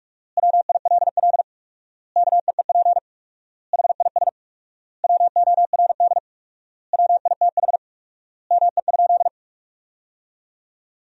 Morse code, heard as a keyed tone, about 43 wpm, 700 Hz. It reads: WILL KEEP HIS WORD WITH ME?
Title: 4x Word-Spacing - Sentences from Top 300 Words - 45wpm